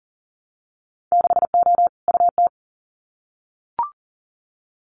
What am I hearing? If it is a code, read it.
Morse code, heard as a keyed tone, about 40 wpm, 700 Hz. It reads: 6O UT
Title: Winter Field Day Exchanges Encore 40wpm